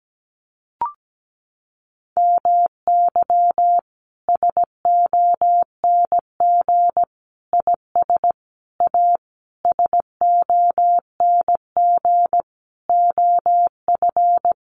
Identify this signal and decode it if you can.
Morse code, heard as a keyed tone, about 17 wpm, 700 Hz. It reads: MY SONG IS A SONG OF
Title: Sentences from Top 1500 Words 17wpm